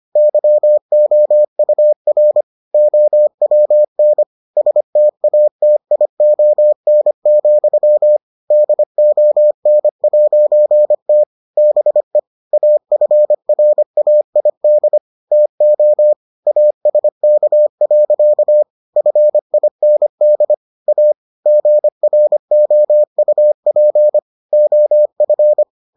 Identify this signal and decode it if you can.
Morse code, heard as a keyed tone, about 25 wpm, 600 Hz. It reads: YOUR OWN STATION, DON'T BE AFRAID TO ASK. FIND A GROUP OF